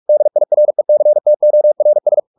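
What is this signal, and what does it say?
Morse code, heard as a keyed tone, about 44 wpm, 600 Hz. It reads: DIRECTORS
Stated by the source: Morse practice at 45 wpm, the novel Never by Ken Follett in English, from CW med Gnister.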